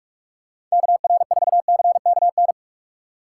Morse code, heard as a keyed tone, about 46 wpm, 700 Hz. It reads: KR4CKN
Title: USA Call Signs - 6 Characters - 3x Character Spacing - 45wpm